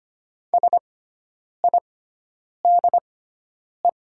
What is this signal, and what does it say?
Morse code, heard as a keyed tone, about 25 wpm, spaced as for 9 wpm, 700 Hz. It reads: SIDE
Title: 6x Character Spacing - Top 100 Words - 25wpm